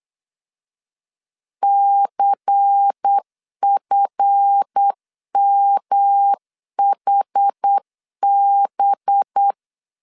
Morse code, heard as a keyed tone, about 8 wpm, 800 Hz. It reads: CFMHB
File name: LEZIONI / 7 GF9Z / 7 GF 8wpm 3108